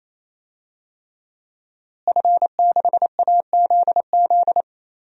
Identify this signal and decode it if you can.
Morse code, heard as a keyed tone, about 28 wpm, 700 Hz. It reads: F6AZZ
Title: DXCC-100 Call Signs - 5 Characters - Repeated 3x - 28wpm